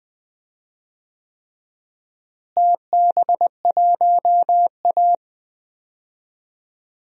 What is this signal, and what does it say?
Morse code, heard as a keyed tone, about 20 wpm, 700 Hz. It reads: TB1A
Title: DXCC-100 Call Signs - 4 Characters - Repeated 3x - 20wpm